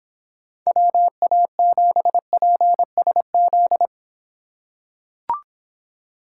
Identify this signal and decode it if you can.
Morse code, heard as a keyed tone, about 26 wpm, 700 Hz. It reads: WA7PSZ